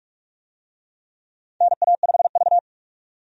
Morse code, heard as a keyed tone, about 46 wpm, 700 Hz. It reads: NA5V
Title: USA Call Signs - 4 Characters - 45wpm